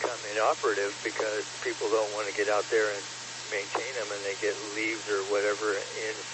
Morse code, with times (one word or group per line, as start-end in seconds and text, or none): none